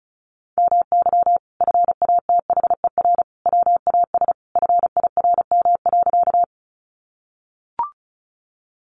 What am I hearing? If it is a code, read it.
Morse code, heard as a keyed tone, about 34 wpm, 700 Hz. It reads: MY FATHER WAS FIRM.
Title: Sentences from Top 1400 Words 35wpm